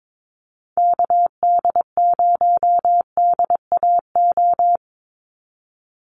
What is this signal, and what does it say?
Morse code, heard as a keyed tone, about 22 wpm, 700 Hz. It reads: KD0DAO
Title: USA Call Signs 22wpm